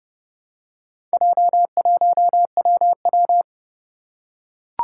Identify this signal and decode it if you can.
Morse code, heard as a keyed tone, about 28 wpm, 700 Hz. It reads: J1WW
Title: Four Challenging Letter-Number Combinations 30wpm